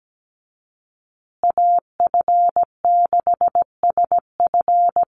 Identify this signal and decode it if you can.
Morse code, heard as a keyed tone, about 17 wpm, 700 Hz. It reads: AF6SF
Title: USA Call Signs Repeated 3x 17wpm